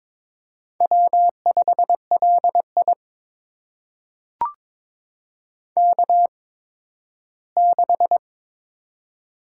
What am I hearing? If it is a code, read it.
Morse code, heard as a keyed tone, about 22 wpm, spaced as for 13 wpm, 700 Hz. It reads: W5LI K6
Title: USA Call Signs - 4 Characters - 8x Character Spacing - 22wpm